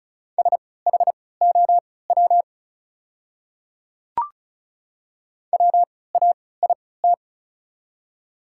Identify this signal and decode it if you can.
Morse code, heard as a keyed tone, about 34 wpm, spaced as for 20 wpm, 690 Hz. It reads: SHOW WAIT
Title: 3x Character Spacing - Top 500 Words - 35wpm